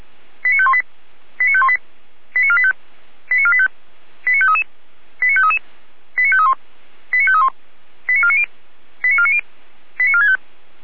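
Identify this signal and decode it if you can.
Five-tone selective call (ZVEI), heard as quick runs of five tones, tones of about 70 ms. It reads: 89518 89518 89485 89485 89533 89533 89531 89531 89490 89490 89465
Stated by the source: ZVEI five-tone selective call received on an SDR.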